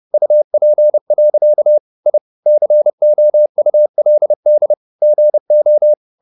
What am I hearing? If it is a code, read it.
Morse code, heard as a keyed tone, about 30 wpm, 600 Hz. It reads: UP. I COULD GO
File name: the old man and the sea / the old man and the sea 1 30wpm